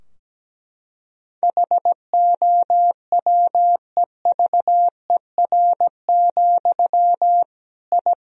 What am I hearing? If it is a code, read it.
Morse code, heard as a keyed tone, about 17 wpm, 700 Hz. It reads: HOWEVER, I
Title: Sentences from Top 1000 Words - Review Part 3 - 17wpm